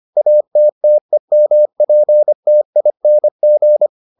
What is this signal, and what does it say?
Morse code, heard as a keyed tone, about 25 wpm, 600 Hz. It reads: ATTEMPTING